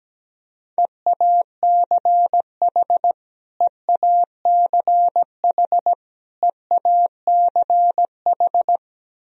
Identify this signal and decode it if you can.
Morse code, heard as a keyed tone, about 17 wpm, 700 Hz. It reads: EACH EACH EACH
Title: Top 100 Words Repeated 17wpm